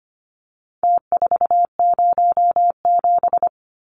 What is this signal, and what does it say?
Morse code, heard as a keyed tone, about 25 wpm, 700 Hz. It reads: T407
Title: Numbers - Contest Style - 4-Digit 25wpm